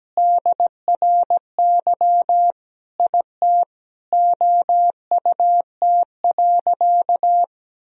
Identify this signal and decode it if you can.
Morse code, heard as a keyed tone, about 17 wpm, 700 Hz. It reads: DRY IT OUT.
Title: Sentences from Top 1000 Words - Review Part 3 - 17wpm